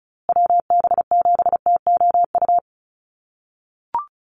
Morse code, heard as a keyed tone, about 34 wpm, 700 Hz. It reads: WB7TOU
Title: USA Call Signs Repeated 2x 35wpm